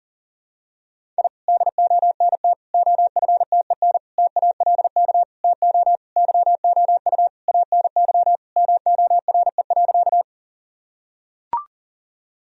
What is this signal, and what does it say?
Morse code, heard as a keyed tone, about 40 wpm, 700 Hz. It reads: I DONT OFTEN TALK TO YOU ANY MORE.
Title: Sentences from Top 1000 Words - Review Part 1 - 40wpm